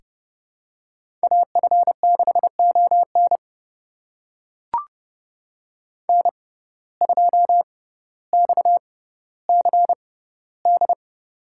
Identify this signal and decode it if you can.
Morse code, heard as a keyed tone, about 32 wpm, spaced as for 16 wpm, 690 Hz. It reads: AF6ON N2XCD
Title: USA Call Signs - 5 Characters - 6x Character Spacing - 30wpm